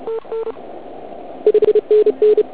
Morse code, heard as a keyed tone, about 35 wpm, 440 Hz. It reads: TN 5NN
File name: Archiv / CUC106 / cuc106 PILEUP